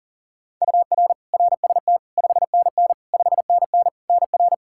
Morse code, heard as a keyed tone, about 40 wpm, 700 Hz. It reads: UR RST 5NN 5NN NR
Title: Practice QSO - Exchange 1 - Beginning - 40wpm